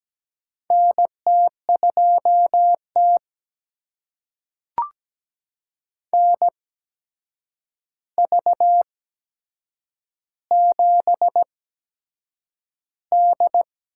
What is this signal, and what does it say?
Morse code, heard as a keyed tone, about 17 wpm, spaced as for 7 wpm, 700 Hz. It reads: NT2T NV7D
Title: USA Call Signs - 4 Characters - 8x Character Spacing - 17wpm